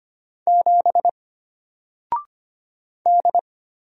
Morse code, heard as a keyed tone, about 25 wpm, 700 Hz. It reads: 7 D
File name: Single-Letter-Number-25wpm